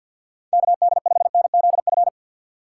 Morse code, heard as a keyed tone, about 51 wpm, 700 Hz. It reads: KD5NZF
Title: USA Call Signs - 6 Characters - 4x Character Spacing - 50wpm